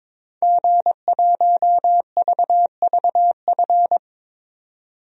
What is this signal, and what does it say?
Morse code, heard as a keyed tone, about 22 wpm, 700 Hz. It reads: G1VVF
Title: DXCC-100 Call Signs - 5 Characters - 22wpm